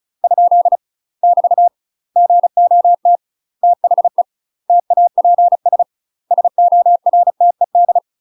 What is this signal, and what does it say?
Morse code, heard as a keyed tone, about 34 wpm, 700 Hz. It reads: ? <BT> GOT THE TAPS SORTED
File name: UK MASTER QSOs TRAINING / 35WPM / UK MASTER QSOs 0015